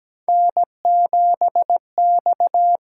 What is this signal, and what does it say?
Morse code, heard as a keyed tone, about 17 wpm, 700 Hz. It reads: N7X
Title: USA Call Signs - 8x Character Spacing - 17wpm